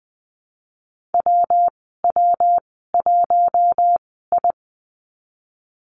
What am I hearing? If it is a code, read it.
Morse code, heard as a keyed tone, about 20 wpm, spaced as for 15 wpm, 700 Hz. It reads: WW1I
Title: 200 Lesson 32g - Letter k - Call Signs 20wpm